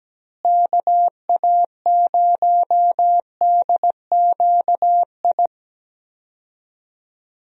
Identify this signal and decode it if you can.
Morse code, heard as a keyed tone, about 17 wpm, 700 Hz. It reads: KA0DQI
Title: USA Call Signs - 6 Characters - Repeated 2x - 17wpm